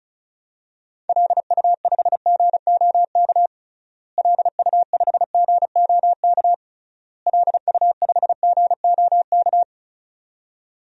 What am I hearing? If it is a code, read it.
Morse code, heard as a keyed tone, about 36 wpm, 700 Hz. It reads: LU5GOK LU5GOK LU5GOK
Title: DXCC-100 Call Signs - 6 Characters - Repeated 3x - 35wpm